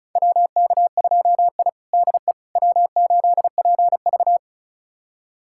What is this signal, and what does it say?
Morse code, heard as a keyed tone, about 34 wpm, 700 Hz. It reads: WK2I DE W8PV
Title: USA Callsign DE Callsign - 4 Characters - 35wpm